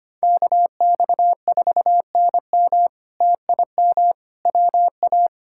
Morse code, heard as a keyed tone, about 25 wpm, 700 Hz. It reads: KX4NM TIM WA
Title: K1USN SST QSO Practice - 25wpm